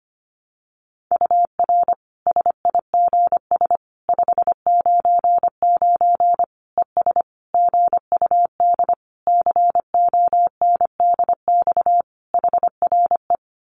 Morse code, heard as a keyed tone, about 25 wpm, 700 Hz. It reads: UR SIGS 599 ES GUD CONDX HRE
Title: Common QSO Phrases 25wpm